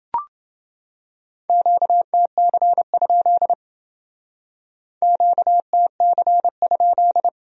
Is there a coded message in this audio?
Yes: Morse code from 1.5 to 7.3 s, heard as a keyed tone, about 29 wpm, 700 Hz. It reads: QTC? QTC?